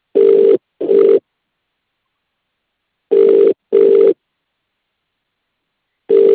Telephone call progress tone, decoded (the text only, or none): ringback tone